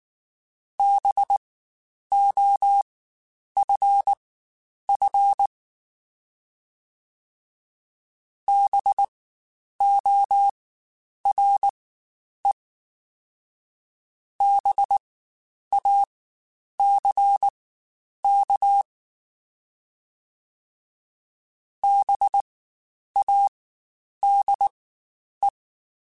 Morse code, heard as a keyed tone, about 19 wpm, spaced as for 9 wpm, 800 Hz. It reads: BOFF BORE BACK BADE
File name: Spaced Alphas / Spaced Alphas mp3s / Bs sp @18